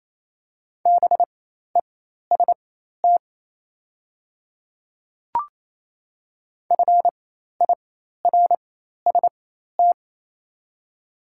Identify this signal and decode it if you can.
Morse code, heard as a keyed tone, about 27 wpm, spaced as for 13 wpm, 690 Hz. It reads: BEST FIRST
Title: 4x Character Spacing - Top 500 Words - 28wpm